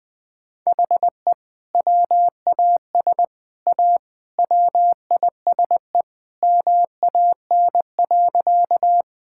Morse code, heard as a keyed tone, about 19 wpm, 700 Hz. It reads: HE WAS A WISE MAN.
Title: Sentences from Top 1600 Words 20wpm